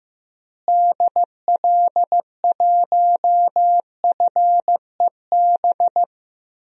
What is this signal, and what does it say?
Morse code, heard as a keyed tone, about 15 wpm, 700 Hz. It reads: DL1FEB
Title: DXCC-100 Call Signs - 6 Characters - Repeated 3x - 15wpm